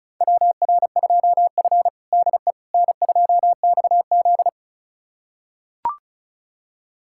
Morse code, heard as a keyed tone, about 36 wpm, 700 Hz. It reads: WR2F DE N2XZ
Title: USA Callsign DE Callsign - 4 Characters - 35wpm